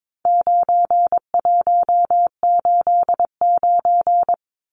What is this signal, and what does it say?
Morse code, heard as a keyed tone, about 22 wpm, 700 Hz. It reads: 9189
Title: Numbers - 4-Digit 22wpm